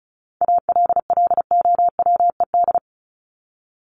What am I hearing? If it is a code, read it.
Morse code, heard as a keyed tone, about 34 wpm, 700 Hz. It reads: ALLOWED
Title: ED Suffix - Top 100 Words with Base Words 35wpm